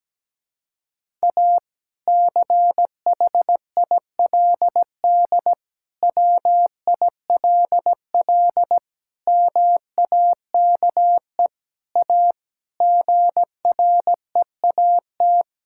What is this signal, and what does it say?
Morse code, heard as a keyed tone, about 17 wpm, 700 Hz. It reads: A CHILD WILL MAKE A GREAT
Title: Sentences from Top 1000 Words - Review Part 6 - 17wpm